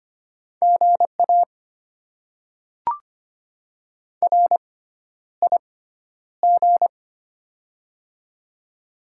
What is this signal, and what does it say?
Morse code, heard as a keyed tone, about 25 wpm, spaced as for 11 wpm, 700 Hz. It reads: GA RIG
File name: Top 90 QSO Elements - 6x Character Spacing - 25wpm